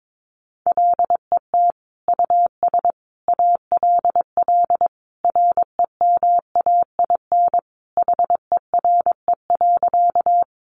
Morse code, heard as a keyed tone, about 22 wpm, 700 Hz. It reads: LET US ALL REMAIN HERE.
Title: Sentences from Top 1200 Words 22wpm